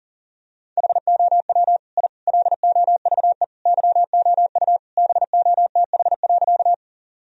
Morse code, heard as a keyed tone, about 40 wpm, 700 Hz. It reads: HOW I LOVE YOU BOTH.
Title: Sentences from Top 1000 Words - Review Part 1 - 40wpm